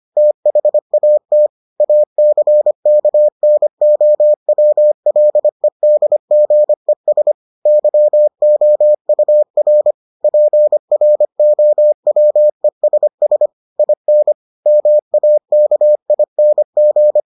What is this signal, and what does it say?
Morse code, heard as a keyed tone, about 25 wpm, 600 Hz. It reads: THAT ACKNOWLEDGES YOUR PROWESS IN MAKING